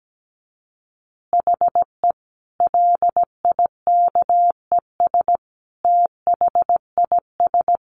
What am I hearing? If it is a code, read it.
Morse code, heard as a keyed tone, about 17 wpm, 700 Hz. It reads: HE LIKES THIS
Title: Additional Easy Phrases 17wpm